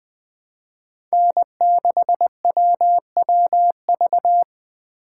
Morse code, heard as a keyed tone, about 20 wpm, 700 Hz. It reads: N6WWV
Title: USA Call Signs - 5 Characters - 3x Character Spacing - 20wpm